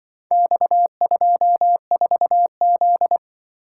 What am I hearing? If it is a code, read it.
Morse code, heard as a keyed tone, about 24 wpm, 700 Hz. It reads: X24Z
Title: Speed-Racing - Four Letter-Number Combinations - 20wpm